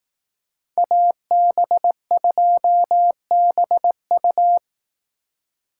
Morse code, heard as a keyed tone, about 18 wpm, 700 Hz. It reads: AB2BU
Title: Speed Racing - USA Call Signs 15wpm